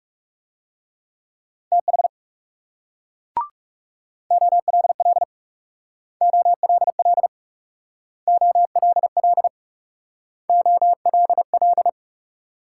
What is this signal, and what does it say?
Morse code, heard as a keyed tone, about 37 wpm, 700 Hz. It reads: TH OLL OLL OLL OLL
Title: Speed-Racing Multi-letter Phonemes - 30wpm